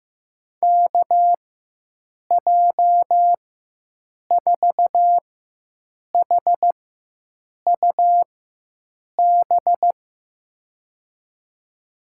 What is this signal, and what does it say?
Morse code, heard as a keyed tone, about 15 wpm, spaced as for 7 wpm, 700 Hz. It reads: KJ4HUB
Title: USA Call Signs - 4x Character Spacing - 15wpm